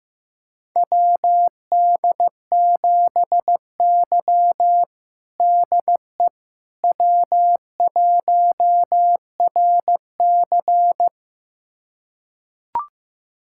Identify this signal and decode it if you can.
Morse code, heard as a keyed tone, about 15 wpm, 700 Hz. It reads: WD7Y DE W1RC
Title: USA Callsign DE Callsign - 4 Characters - 15wpm